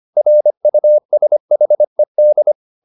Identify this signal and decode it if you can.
Morse code, heard as a keyed tone, about 25 wpm, 600 Hz. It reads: RUSHED